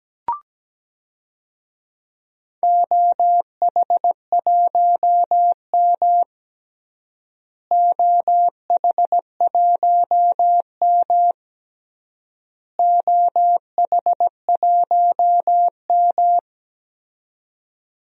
Morse code, heard as a keyed tone, about 17 wpm, 700 Hz. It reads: OH1M OH1M OH1M
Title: DXCC-100 Call Signs - 4 Characters - Repeated 3x - 17wpm